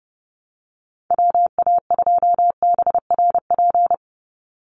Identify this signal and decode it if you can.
Morse code, heard as a keyed tone, about 30 wpm, 700 Hz. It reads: WA2BRP
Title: USA Call Signs 30wpm